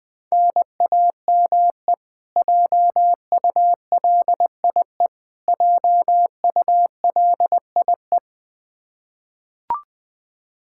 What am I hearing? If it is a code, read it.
Morse code, heard as a keyed tone, about 20 wpm, 700 Hz. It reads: NAME JULIE JULIE
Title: Practice QSO - Exchange 1 - Beginning - 20wpm